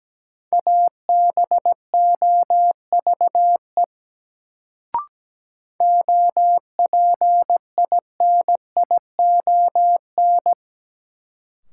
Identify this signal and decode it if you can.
Morse code, heard as a keyed tone, about 17 wpm, 700 Hz. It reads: ABOVE OPINION
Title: Sentences from Top 1200 Words - New Word Drill Alt - 17wpm